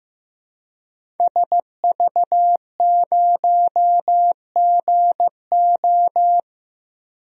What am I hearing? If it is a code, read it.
Morse code, heard as a keyed tone, about 15 wpm, 700 Hz. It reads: SV0GO
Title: DXCC-100 Call Signs - 5 Characters - 15wpm